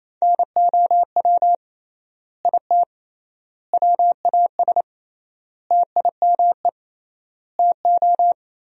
Morse code, heard as a keyed tone, about 28 wpm, 700 Hz. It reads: NOW IT WAS TIME TO
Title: 3x Word-Spacing - Sentences from Top 200 Words - 28wpm